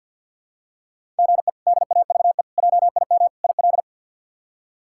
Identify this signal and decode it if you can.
Morse code, heard as a keyed tone, about 53 wpm, 700 Hz. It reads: GE DAVE JIM IL
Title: K1USN SST QSO Practice - 50wpm